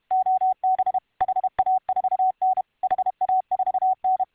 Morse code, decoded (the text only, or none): OB HA4N HA4N